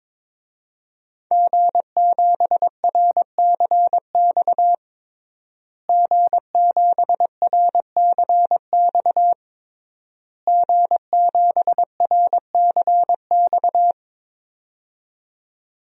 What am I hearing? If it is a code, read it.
Morse code, heard as a keyed tone, about 22 wpm, 700 Hz. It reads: G7RCX G7RCX G7RCX
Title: DXCC-100 Call Signs - 5 Characters - Repeated 3x - 22wpm